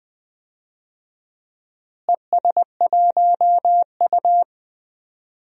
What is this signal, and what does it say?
Morse code, heard as a keyed tone, about 20 wpm, 700 Hz. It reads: ES1U
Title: DXCC-100 Call Signs - 4 Characters - Repeated 2x - 20wpm